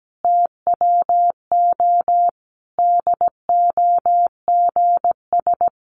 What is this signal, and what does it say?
Morse code, heard as a keyed tone, about 17 wpm, 700 Hz. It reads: TWO DOGS
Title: Easy Phrases 17wpm